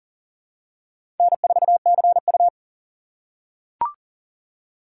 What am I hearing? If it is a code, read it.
Morse code, heard as a keyed tone, about 40 wpm, 700 Hz. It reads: N4CU
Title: USA Call Signs - 4 Characters - 4x Character Spacing - 40wpm